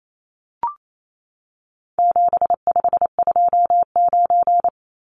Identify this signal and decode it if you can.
Morse code, heard as a keyed tone, about 28 wpm, 700 Hz. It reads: 7529
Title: Numbers - 4-Digit 28wpm